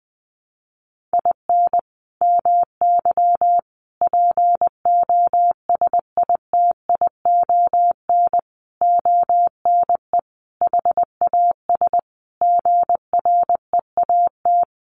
Morse code, heard as a keyed tone, about 20 wpm, 700 Hz. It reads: IN MY POSITION ONE HAS GREAT
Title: Sentences from Top 1100 Words 20wpm